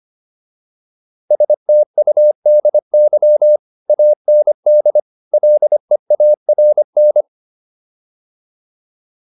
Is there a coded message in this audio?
Yes: Morse code from 1.3 to 7.2 s, heard as a keyed tone, about 25 wpm, 600 Hz. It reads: STUDY AND LEARN